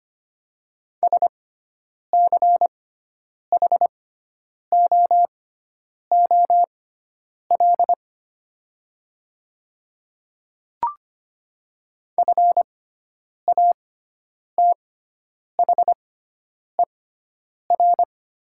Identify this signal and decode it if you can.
Morse code, heard as a keyed tone, about 25 wpm, spaced as for 9 wpm, 700 Hz. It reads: SCHOOL FATHER
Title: 6x Character Spacing - Top 300 Words - 25wpm